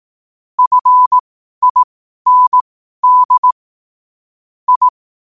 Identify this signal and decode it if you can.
Morse code, heard as a keyed tone, about 18 wpm, spaced as for 12 wpm, 1000 Hz. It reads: FIND I